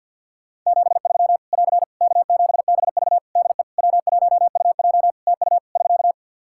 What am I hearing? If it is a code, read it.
Morse code, heard as a keyed tone, about 50 wpm, 700 Hz. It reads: 73 <AR> K7BV DE W1UJ TU <SK>
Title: K1USN SST QSO Practice 50wpm